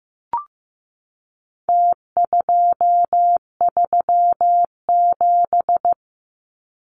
Morse code, heard as a keyed tone, about 15 wpm, 700 Hz. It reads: T237
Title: Numbers - Contest Style - 4-Digit 15wpm